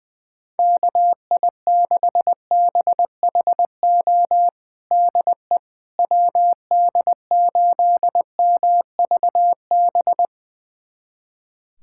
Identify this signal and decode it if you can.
Morse code, heard as a keyed tone, about 20 wpm, 700 Hz. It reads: KI6BHO DE WD8MVB
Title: USA Callsign DE Callsign 20wpm